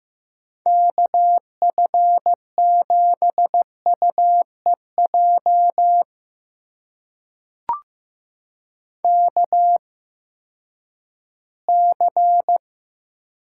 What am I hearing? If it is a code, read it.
Morse code, heard as a keyed tone, about 15 wpm, spaced as for 10 wpm, 700 Hz. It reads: KF7UEJ KC
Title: USA Call Signs - 6 Characters - 8x Character Spacing - 15wpm